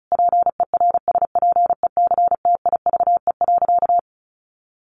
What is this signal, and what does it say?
Morse code, heard as a keyed tone, about 34 wpm, 700 Hz. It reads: PERSPECTIVE.
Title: Sentences from Top 1200 Words 35wpm